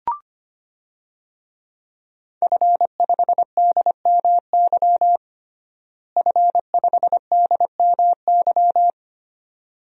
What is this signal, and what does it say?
Morse code, heard as a keyed tone, about 25 wpm, 700 Hz. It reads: F5DMY F5DMY
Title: DXCC-100 Call Signs - 5 Characters - Repeated 2x - 25wpm